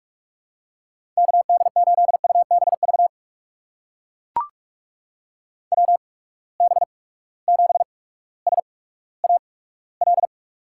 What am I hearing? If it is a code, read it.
Morse code, heard as a keyed tone, about 43 wpm, spaced as for 19 wpm, 700 Hz. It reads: KD8UBV WB7SAL
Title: USA Call Signs - 6 Characters - 8x Character Spacing - 45wpm